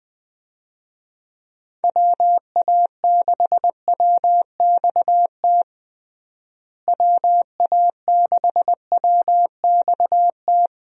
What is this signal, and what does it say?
Morse code, heard as a keyed tone, about 21 wpm, 700 Hz. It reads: WA6WXT WA6WXT
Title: USA Call Signs - 6 Characters - Repeated 2x - 20wpm